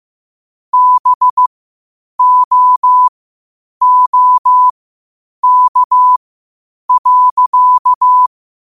Morse code, heard as a keyed tone, about 15 wpm, spaced as for 8 wpm, 1000 Hz. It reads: BOOK.
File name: qotdfiles / 8wpm